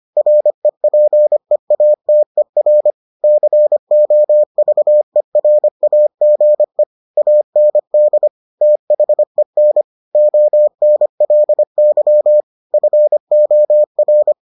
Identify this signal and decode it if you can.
Morse code, heard as a keyed tone, about 25 wpm, 600 Hz. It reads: REPEATER COVERAGE AND THEN ONLY FOR